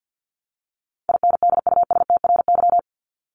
Morse code, heard as a keyed tone, about 50 wpm, 700 Hz. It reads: INDUSTRY
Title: Top 501-750 Words 50wpm